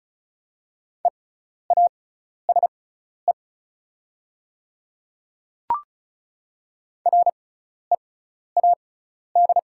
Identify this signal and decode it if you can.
Morse code, heard as a keyed tone, about 34 wpm, spaced as for 12 wpm, 690 Hz. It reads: EASE READ
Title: 6x Character Spacing - Top 500 Words - 35wpm